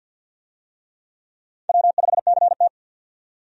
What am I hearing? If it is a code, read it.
Morse code, heard as a keyed tone, about 49 wpm, 700 Hz. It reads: W5CT
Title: USA Call Signs - 4 Characters - 50wpm